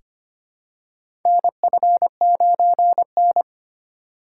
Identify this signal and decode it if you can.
Morse code, heard as a keyed tone, about 25 wpm, 700 Hz. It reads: NF9N
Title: USA Call Signs - 4 Characters - 3x Character Spacing - 25wpm